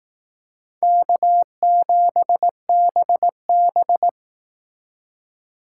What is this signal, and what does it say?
Morse code, heard as a keyed tone, about 18 wpm, 700 Hz. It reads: K7BB
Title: Speed Racing - USA 1x2 Call Signs 15wpm